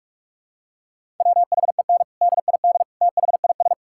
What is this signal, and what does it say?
Morse code, heard as a keyed tone, about 46 wpm, 700 Hz. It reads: WHEN DID THIS
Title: Sentences from Top 1000 Words - Review Part 6 - 45wpm